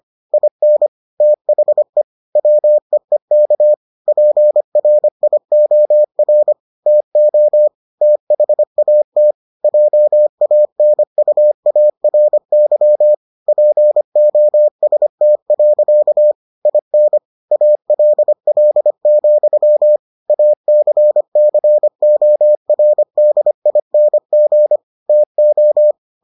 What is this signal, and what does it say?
Morse code, heard as a keyed tone, about 25 wpm, 600 Hz. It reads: IN THE WEEK PRIOR TO THAT JANUARY POST. IN ALL, ACCORDING TO